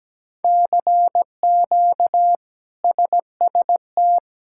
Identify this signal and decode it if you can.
Morse code, heard as a keyed tone, about 17 wpm, 700 Hz. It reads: CQ SST
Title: K1USN SST QSO Practice - 17wpm